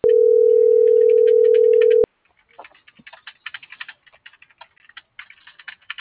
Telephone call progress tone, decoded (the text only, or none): ringback tone